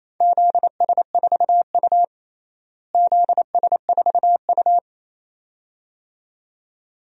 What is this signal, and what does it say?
Morse code, heard as a keyed tone, about 28 wpm, 700 Hz. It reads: ZS4U ZS4U
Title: DXCC-100 Call Signs - 4 Characters - Repeated 2x - 28wpm